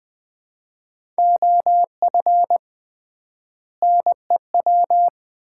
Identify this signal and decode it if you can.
Morse code, heard as a keyed tone, about 20 wpm, 700 Hz. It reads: OF NEW
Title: 3x Word-Spacing - Sets of 3 Words - 20wpm